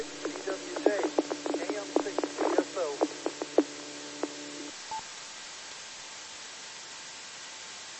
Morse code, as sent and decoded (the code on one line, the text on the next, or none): none